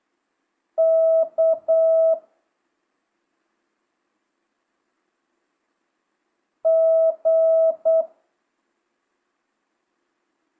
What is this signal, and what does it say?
Morse code, heard as a keyed tone, about 8 wpm, 650 Hz. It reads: K G
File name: jh8cbh / 2025cw